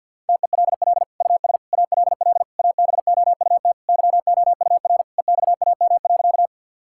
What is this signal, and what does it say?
Morse code, heard as a keyed tone, about 50 wpm, 700 Hz. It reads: TELL US ALL ABOUT YOUR EXAM.
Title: Sentences from Top 1000 Words - Review Part 5 - 50wpm